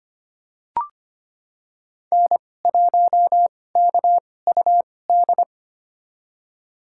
Morse code, heard as a keyed tone, about 25 wpm, spaced as for 18 wpm, 700 Hz. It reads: N1KUD
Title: USA Call Signs - 5 Characters - 2x Character Spacing - 25wpm